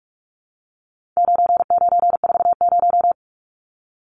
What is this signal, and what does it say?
Morse code, heard as a keyed tone, about 44 wpm, 700 Hz. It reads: 9940
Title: Numbers - 4-Digit 45wpm